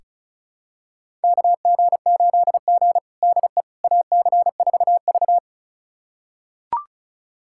Morse code, heard as a keyed tone, about 36 wpm, 700 Hz. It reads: KG8G DE AC4V
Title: USA Callsign DE Callsign - 4 Characters - 35wpm